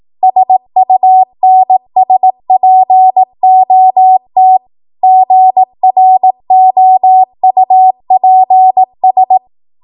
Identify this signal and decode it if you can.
Morse code, heard as a keyed tone, about 18 wpm, 750 Hz. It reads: SUNSPOT GROUPS
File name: Archive / 18 WPM / 18WPM